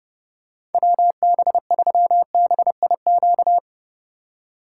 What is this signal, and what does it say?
Morse code, heard as a keyed tone, about 30 wpm, 700 Hz. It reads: WB3BIQ
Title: K1USN SST QSO Practice - 30wpm